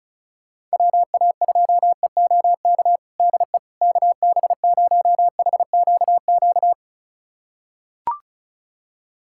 Morse code, heard as a keyed tone, about 34 wpm, 700 Hz. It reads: WA2EOK DE KB0HQQ
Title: USA Callsign DE Callsign - 6 Characters - 35wpm